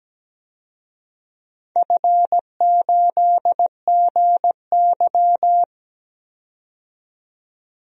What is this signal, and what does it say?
Morse code, heard as a keyed tone, about 17 wpm, 700 Hz. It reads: F8GY
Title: DXCC-100 Call Signs - 4 Characters - 17wpm